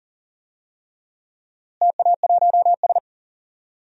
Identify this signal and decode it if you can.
Morse code, heard as a keyed tone, about 40 wpm, 700 Hz. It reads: TA1S
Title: DXCC-100 Call Signs - 4 Characters - Repeated 3x - 40wpm